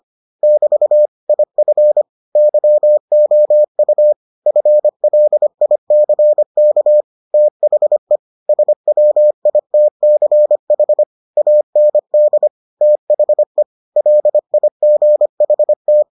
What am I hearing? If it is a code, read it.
Morse code, heard as a keyed tone, about 25 wpm, 600 Hz. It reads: <BT> IF YOU FLICK THE SWITCH AND THE LIGHT